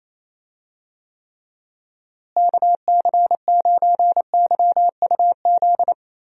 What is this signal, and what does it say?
Morse code, heard as a keyed tone, about 28 wpm, 700 Hz. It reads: KC9YUZ
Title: USA Call Signs - 6 Characters - Repeated 2x - 28wpm